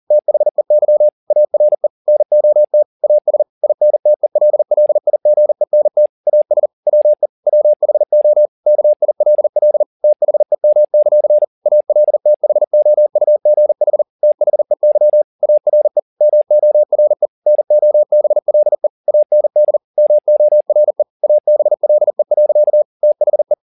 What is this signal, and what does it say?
Morse code, heard as a keyed tone, about 40 wpm, 600 Hz. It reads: THEY ARE NOT AS INTELLIGENT AS WE WHO KILL THEM; ALTHOUGH THEY ARE MORE NOBLE AND MORE ABLE. THE